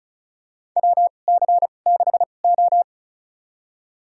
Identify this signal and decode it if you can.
Morse code, heard as a keyed tone, about 34 wpm, spaced as for 25 wpm, 690 Hz. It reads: WC6O
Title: USA Call Signs - 4 Characters - 2x Character Spacing - 35wpm